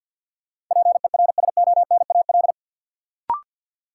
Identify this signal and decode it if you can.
Morse code, heard as a keyed tone, about 47 wpm, 700 Hz. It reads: PERSONAL
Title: Sentences from Top 700 Words - New Word Drill - 50wpm